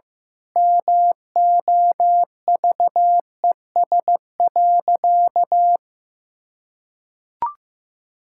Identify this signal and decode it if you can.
Morse code, heard as a keyed tone, about 15 wpm, 700 Hz. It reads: MOVES.
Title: Pangram Challenge v2 - 15wpm